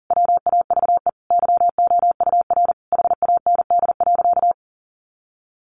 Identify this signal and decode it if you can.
Morse code, heard as a keyed tone, about 40 wpm, 700 Hz. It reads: WAVE YOUR HAND.
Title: Sentences from Top 1400 Words 40wpm